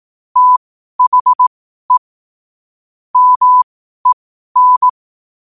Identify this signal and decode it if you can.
Morse code, heard as a keyed tone, about 18 wpm, spaced as for 12 wpm, 1000 Hz. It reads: THE MEN
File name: qotdfiles / 12wpm